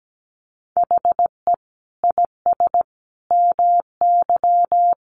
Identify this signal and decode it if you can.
Morse code, heard as a keyed tone, about 17 wpm, 700 Hz. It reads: HE IS MY
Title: Sentences from Top 1400 Words 17wpm